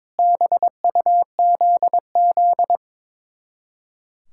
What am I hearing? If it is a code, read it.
Morse code, heard as a keyed tone, about 23 wpm, 700 Hz. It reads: BUZZ
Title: Speed-Racing Common QSO Phrases - 22wpm